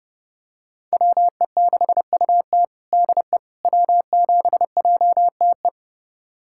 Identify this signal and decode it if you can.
Morse code, heard as a keyed tone, about 32 wpm, 700 Hz. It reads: WE6UT DE W7JTE
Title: USA Callsign DE Callsign - 5 Characters - 30wpm